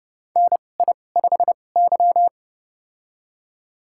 Morse code, heard as a keyed tone, about 32 wpm, spaced as for 22 wpm, 700 Hz. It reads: NI5Y
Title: USA Call Signs - 4 Characters - 2x Character Spacing - 30wpm